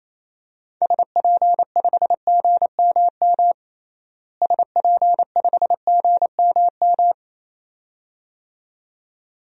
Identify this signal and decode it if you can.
Morse code, heard as a keyed tone, about 28 wpm, 700 Hz. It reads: SP5GMM SP5GMM
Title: DXCC-100 Call Signs - 6 Characters - Repeated 2x - 28wpm